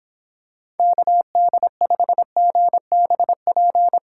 Morse code, heard as a keyed tone, about 26 wpm, 700 Hz. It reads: KD5GBP